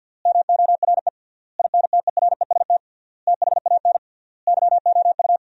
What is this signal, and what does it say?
Morse code, heard as a keyed tone, about 50 wpm, 700 Hz. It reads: MORE INTEREST THAN YOU
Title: 3x Word-Spacing - Sentences from Top 400 Words - 50wpm